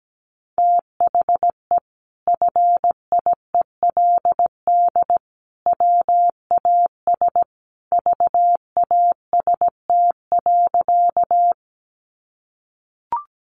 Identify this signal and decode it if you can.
Morse code, heard as a keyed tone, about 17 wpm, 700 Hz. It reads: THE FIELD WAS VAST.
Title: Sentences from Top 1300 Words 17wpm